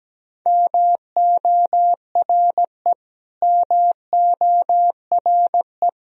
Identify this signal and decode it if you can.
Morse code, heard as a keyed tone, about 17 wpm, 700 Hz. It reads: MORE MORE
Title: Top 100 Words Repeated 17wpm